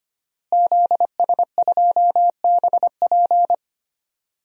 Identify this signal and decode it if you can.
Morse code, heard as a keyed tone, about 25 wpm, 700 Hz. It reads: ZS2BP
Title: DXCC-100 Call Signs - 5 Characters - Repeated 2x - 25wpm